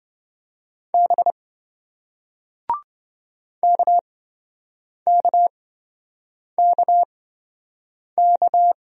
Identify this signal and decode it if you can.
Morse code, heard as a keyed tone, about 27 wpm, spaced as for 7 wpm, 700 Hz. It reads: B KKKK